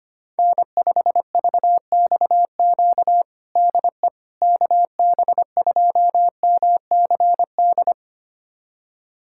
Morse code, heard as a keyed tone, about 25 wpm, 700 Hz. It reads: N5VXQ DE KB2MCD
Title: USA Callsign DE Callsign 25wpm